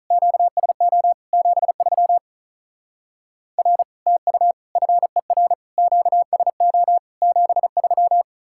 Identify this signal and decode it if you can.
Morse code, heard as a keyed tone, about 43 wpm, 700 Hz. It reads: QSO 73 R TU FER QSO 73
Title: Speed-Racing Common QSO Phrases - 35wpm